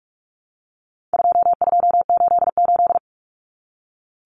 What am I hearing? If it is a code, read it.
Morse code, heard as a keyed tone, about 44 wpm, 700 Hz. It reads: 2288
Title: Common Cut Numbers - 4-Digit 45wpm